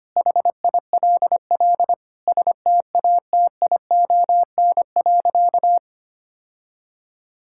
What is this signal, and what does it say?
Morse code, heard as a keyed tone, about 25 wpm, 700 Hz. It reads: HILL STATION.
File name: practice / 25wpm / Difficult Sentences with Hints - 25wpm